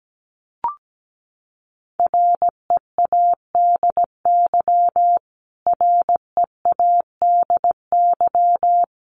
Morse code, heard as a keyed tone, about 17 wpm, 700 Hz. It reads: READY READY
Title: Top 500 Words Repeated 17wpm